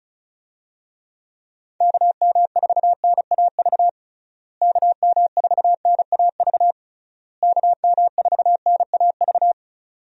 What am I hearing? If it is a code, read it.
Morse code, heard as a keyed tone, about 34 wpm, 700 Hz. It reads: KM4NAV KM4NAV KM4NAV
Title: USA Call Signs - 6 Characters - Repeated 3x - 35wpm